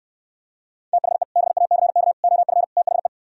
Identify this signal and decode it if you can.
Morse code, heard as a keyed tone, about 68 wpm, 700 Hz. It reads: THE BACK OF THE